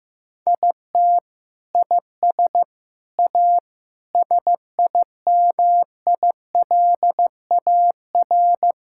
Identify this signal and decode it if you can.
Morse code, heard as a keyed tone, about 15 wpm, 700 Hz. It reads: IT IS A SIMILAR
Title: Sentences from Top 1000 Words - Review Part 5 - 15wpm